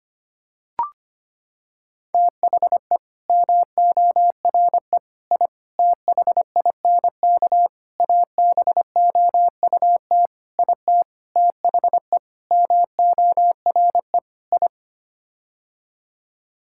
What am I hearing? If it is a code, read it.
Morse code, heard as a keyed tone, about 25 wpm, 700 Hz. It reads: THE MORE I THINK ABOUT IT THE MORE I
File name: Sets of 9 Words - 25wpm